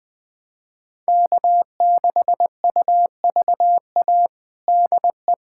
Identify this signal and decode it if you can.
Morse code, heard as a keyed tone, about 20 wpm, 700 Hz. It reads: K6UVA DE
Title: USA Callsign DE Callsign - 5 Characters - 20wpm